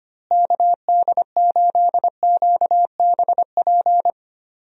Morse code, heard as a keyed tone, about 25 wpm, 700 Hz. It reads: KD8QBP
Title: USA Call Signs - 6 Characters - Repeated 3x - 25wpm